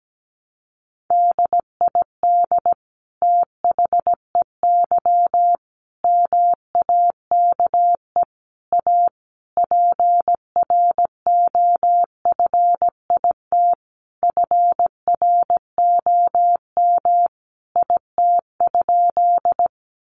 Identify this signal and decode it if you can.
Morse code, heard as a keyed tone, about 17 wpm, 700 Hz. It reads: DID THEY MAKE A PROFIT FROM IT?
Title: Sentences from Top 800 Words 17wpm